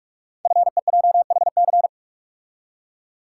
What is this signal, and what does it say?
Morse code, heard as a keyed tone, about 44 wpm, 700 Hz. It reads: FEJHC
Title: Code Groups - 5 Letters - 45wpm